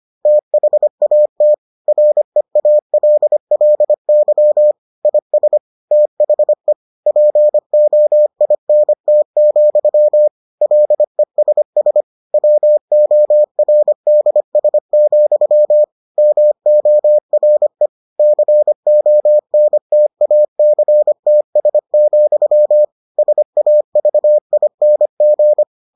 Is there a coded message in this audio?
Yes: Morse code from 0.3 to 25.6 s, heard as a keyed tone, about 25 wpm, 600 Hz. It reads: THAT REALLY IS THE POINT, LESS WORDS, MORE CONTACTS, SAVING